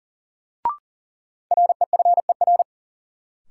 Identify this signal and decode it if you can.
Morse code, heard as a keyed tone, about 40 wpm, 700 Hz. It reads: REFER
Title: Sentences from Top 1000 Words - New Word Drill - 40wpm